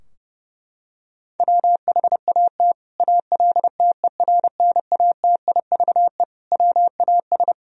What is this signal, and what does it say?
Morse code, heard as a keyed tone, about 30 wpm, 700 Hz. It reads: WHAT ALTERNATIVE WAS
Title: Sentences from Top 1000 Words - Review Part 4 - 30wpm